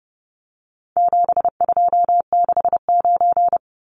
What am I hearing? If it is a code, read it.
Morse code, heard as a keyed tone, about 30 wpm, 700 Hz. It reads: 7269
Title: Numbers - 4-Digit 30wpm